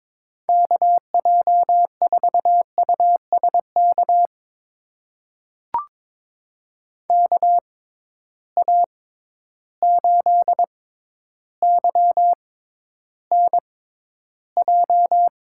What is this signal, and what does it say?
Morse code, heard as a keyed tone, about 22 wpm, spaced as for 11 wpm, 700 Hz. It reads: KJ4USK KA8YNJ
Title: USA Call Signs - 6 Characters - 6x Character Spacing - 22wpm